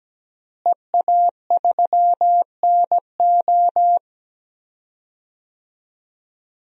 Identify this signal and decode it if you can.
Morse code, heard as a keyed tone, about 17 wpm, 700 Hz. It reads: EA3NO
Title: DXCC-100 Call Signs - 5 Characters - Repeated 2x - 17wpm